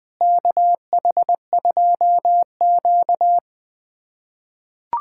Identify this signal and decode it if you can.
Morse code, heard as a keyed tone, about 21 wpm, 700 Hz. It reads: KH2Q
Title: USA Callsign DE Callsign - 4 Characters - 20wpm